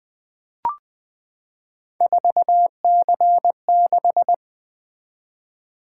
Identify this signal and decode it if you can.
Morse code, heard as a keyed tone, about 20 wpm, 700 Hz. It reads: 4C6
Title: Three Letter-Number Combinations - 20wpm